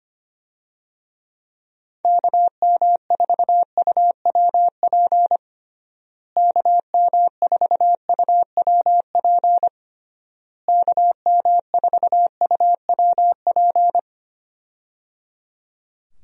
Morse code, heard as a keyed tone, about 25 wpm, 700 Hz. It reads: KM4UWP KM4UWP KM4UWP
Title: USA Call Signs - 6 Characters - Repeated 3x - 25wpm